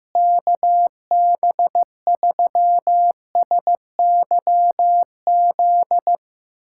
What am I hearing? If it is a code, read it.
Morse code, heard as a keyed tone, about 15 wpm, 700 Hz. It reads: KB3SYZ
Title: USA Callsign DE Callsign 15wpm